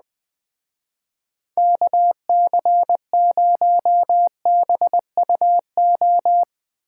Morse code, heard as a keyed tone, about 20 wpm, 700 Hz. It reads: KC0BUO
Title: Speed Racing - USA 2x3 Call Signs 17wpm